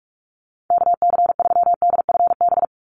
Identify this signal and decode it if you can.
Morse code, heard as a keyed tone, about 44 wpm, 700 Hz. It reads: KC3DFB
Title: USA Call Signs 45wpm